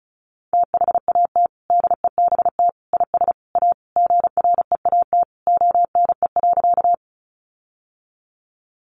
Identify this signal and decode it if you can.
Morse code, heard as a keyed tone, about 34 wpm, 700 Hz. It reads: THAT DEBT IS A GREAT ONE.
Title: Sentences from Top 1300 Words 35wpm